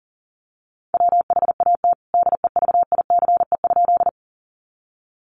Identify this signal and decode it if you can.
Morse code, heard as a keyed tone, about 40 wpm, 700 Hz. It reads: WHAT DEVICE?
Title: Sentences from Top 1500 Words 40wpm